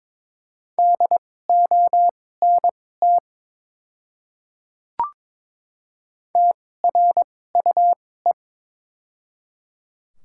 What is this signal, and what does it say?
Morse code, heard as a keyed tone, about 22 wpm, spaced as for 16 wpm, 700 Hz. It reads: DONT TRUE
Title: 2x Character Spacing - Top 500 Words - 22wpm